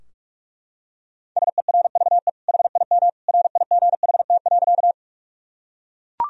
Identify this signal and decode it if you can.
Morse code, heard as a keyed tone, about 41 wpm, 700 Hz. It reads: SERVE HIM RIGHT.
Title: Sentences from Top 1000 Words - Review Part 2 - 45wpm